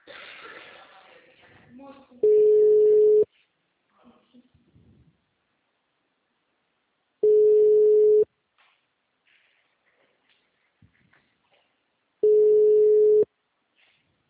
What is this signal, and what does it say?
Ringback tone (Europe, 425 Hz), with speech.